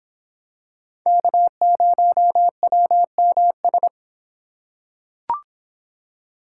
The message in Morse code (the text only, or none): K0WMS